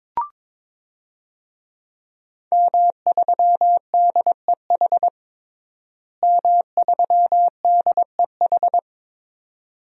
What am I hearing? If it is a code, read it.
Morse code, heard as a keyed tone, about 22 wpm, 700 Hz. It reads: M3DEH M3DEH
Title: DXCC-100 Call Signs - 5 Characters - Repeated 2x - 22wpm